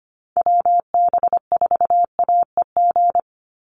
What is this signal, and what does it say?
Morse code, heard as a keyed tone, about 25 wpm, 700 Hz. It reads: WB4AEG
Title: Call Signs - 25wpm